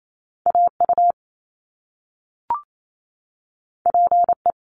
Morse code, heard as a keyed tone, about 28 wpm, 700 Hz. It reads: AU PE
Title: Top Two Letter Combinations in Words 28wpm